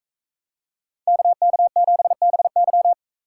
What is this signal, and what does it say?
Morse code, heard as a keyed tone, about 42 wpm, 700 Hz. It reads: KK7BY